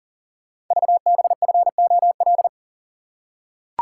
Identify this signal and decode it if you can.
Morse code, heard as a keyed tone, about 40 wpm, 700 Hz. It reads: VBFOL
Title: Code Groups - 5 Letters - 40wpm